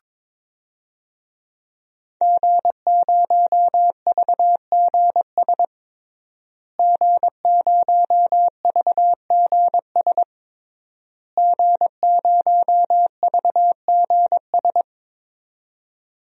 Morse code, heard as a keyed tone, about 22 wpm, 700 Hz. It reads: G0VGS G0VGS G0VGS
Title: DXCC-100 Call Signs - 5 Characters - Repeated 3x - 22wpm